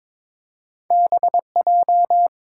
Morse code, heard as a keyed tone, about 23 wpm, 700 Hz. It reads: BJ
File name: practice / 22wpm / Two Letter-Number Combinations - 22wpm